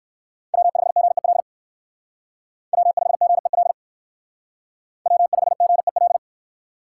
Morse code, heard as a keyed tone, about 56 wpm, 700 Hz. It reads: W5GEL W5GEL W5GEL
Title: Speed Racing - USA 1x3 Call Signs 45wpm